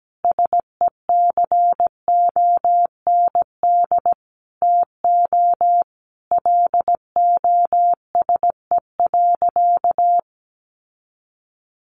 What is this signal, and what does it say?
Morse code, heard as a keyed tone, about 17 wpm, 700 Hz. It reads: SECOND TO LOSE.
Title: Sentences from Top 1100 Words 17wpm